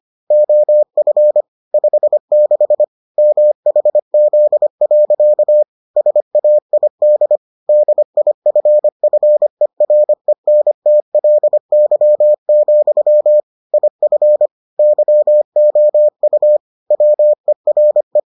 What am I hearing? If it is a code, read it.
Morse code, heard as a keyed tone, about 25 wpm, 600 Hz. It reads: OF 56 MHZ. SAID DIFFERENTLY, IF YOU WERE